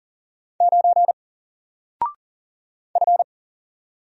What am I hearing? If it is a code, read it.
Morse code, heard as a keyed tone, about 40 wpm, 700 Hz. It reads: 9 F
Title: Single Letter-Number Alt 40wpm